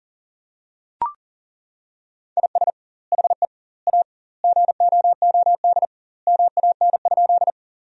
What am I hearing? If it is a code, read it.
Morse code, heard as a keyed tone, about 37 wpm, 700 Hz. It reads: IS HE A GOOD MAN?
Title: 2x Word-Spacing - Sentences from Top 200 Words - 40wpm